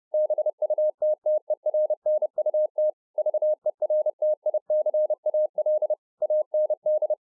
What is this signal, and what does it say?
Morse code, heard as a keyed tone, about 30 wpm, 610 Hz. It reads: BUTTERNUT VERTICAL AND